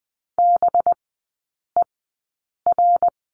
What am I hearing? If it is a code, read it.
Morse code, heard as a keyed tone, about 20 wpm, spaced as for 8 wpm, 700 Hz. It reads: BER
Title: 198 Lesson 32e - Letter k - Three Characters 20wpm